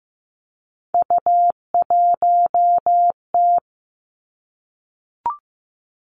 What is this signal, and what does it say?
Morse code, heard as a keyed tone, about 15 wpm, 700 Hz. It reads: U1T
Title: Three Challenging Letter-Number Combinations 15wpm